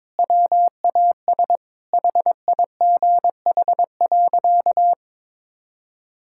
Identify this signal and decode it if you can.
Morse code, heard as a keyed tone, about 22 wpm, 700 Hz. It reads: WAS HIGH.
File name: practice / 22wpm / Sentences with 3-Word Phrases - 22wpm